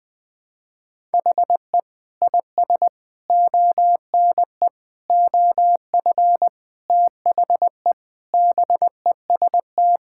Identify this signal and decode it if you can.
Morse code, heard as a keyed tone, about 20 wpm, 700 Hz. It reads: HE IS ONE OF THE BEST
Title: Sentences with 7-Word Phrases - 20wpm